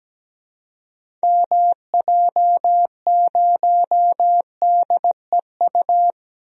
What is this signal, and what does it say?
Morse code, heard as a keyed tone, about 17 wpm, 700 Hz. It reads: MJ0DEU
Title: DXCC-100 Call Signs - 6 Characters - Repeated 2x - 17wpm